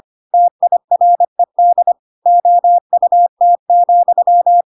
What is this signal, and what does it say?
Morse code, heard as a keyed tone, about 25 wpm, 700 Hz. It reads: TIRED OUT,